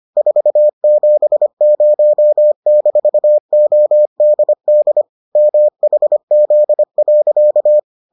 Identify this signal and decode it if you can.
Morse code, heard as a keyed tone, about 25 wpm, 600 Hz. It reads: 470-ODD MHZ.